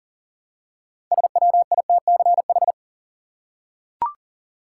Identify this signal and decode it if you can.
Morse code, heard as a keyed tone, about 40 wpm, 700 Hz. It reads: SWITCH
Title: Six Letter Words - 40wpm